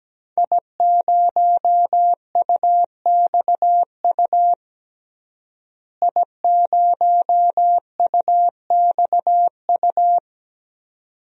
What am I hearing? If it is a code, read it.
Morse code, heard as a keyed tone, about 17 wpm, 700 Hz. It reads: I0UXU I0UXU
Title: DXCC-100 Call Signs - 5 Characters - Repeated 2x - 17wpm